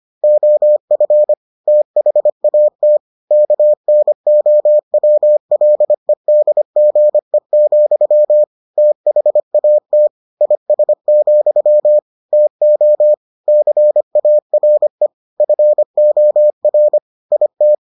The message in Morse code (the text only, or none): OF THAT KNOWLEDGE, THAT IS, TO CARE FOR IT